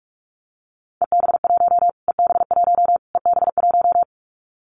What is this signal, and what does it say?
Morse code, heard as a keyed tone, about 46 wpm, 700 Hz. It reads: EB1 EB1 EB1
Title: Callsign Prefixes for Top 25 DXCC - Repeated 3x - 45wpm